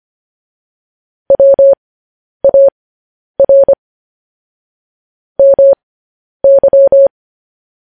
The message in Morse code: WAR MY

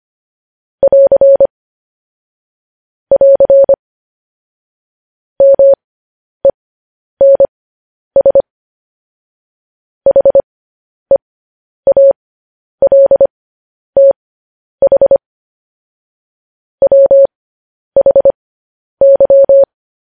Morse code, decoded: <AR> <AR> MENS HEALTH WHY

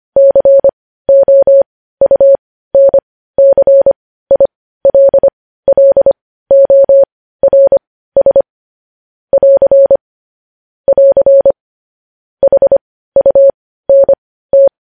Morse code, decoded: COUNCILLORS <AR> <AR> HUNT

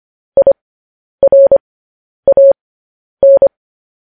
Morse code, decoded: IRAN

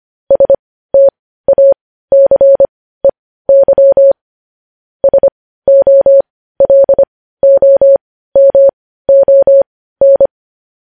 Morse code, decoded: STACEY SOLOMON

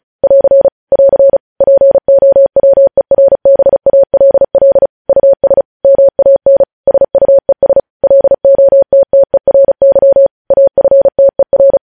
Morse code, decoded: <AR> <AR> POWERBALL US MAN SUES LOTTERY AFTER

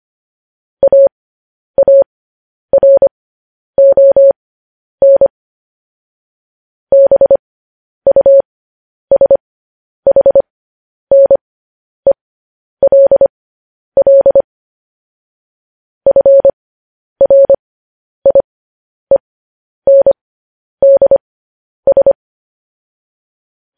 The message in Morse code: AARON BUSHNELL FRIENDS

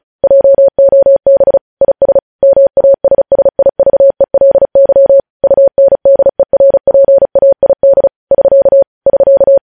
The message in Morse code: JOB IS MASSIVELY UNDERPAID <SK> <SK>

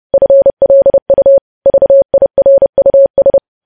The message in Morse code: FLU VIRUS

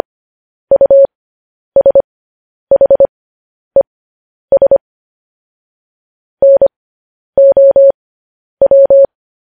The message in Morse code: USHES NOW

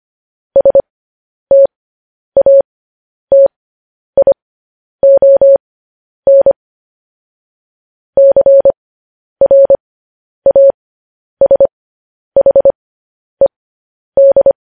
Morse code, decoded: STATION CRASHED

